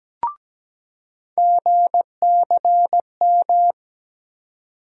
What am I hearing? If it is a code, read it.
Morse code, heard as a keyed tone, about 17 wpm, 700 Hz. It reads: GCM
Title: USA Call Sign Postfixes - 3 Letters 17wpm